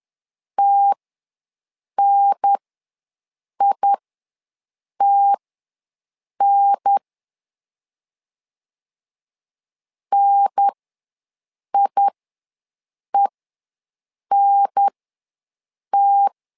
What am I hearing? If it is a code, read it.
Morse code, heard as a keyed tone, about 10 wpm, spaced as for 6 wpm, 800 Hz. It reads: TNITN NIENT
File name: LEZIONI / 2 UHDN / 2 UHDN ws 10wpm 2044